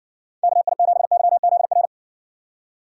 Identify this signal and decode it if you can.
Morse code, heard as a keyed tone, about 60 wpm, 700 Hz. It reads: KI7YZR